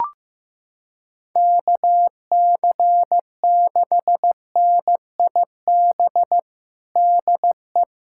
Morse code, heard as a keyed tone, about 15 wpm, 700 Hz. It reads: KC6NIB DE
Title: USA Callsign DE Callsign - 6 Characters - 15wpm